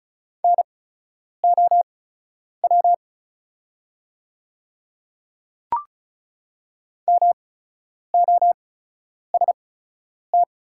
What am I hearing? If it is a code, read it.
Morse code, heard as a keyed tone, about 36 wpm, spaced as for 10 wpm, 700 Hz. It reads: NOW MOST
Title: 8x Character Spacing - Top 100 Words - 35wpm